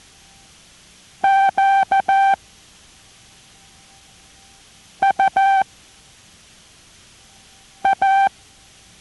Morse code, heard as a keyed tone, about 14 wpm, 780 Hz. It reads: Q U A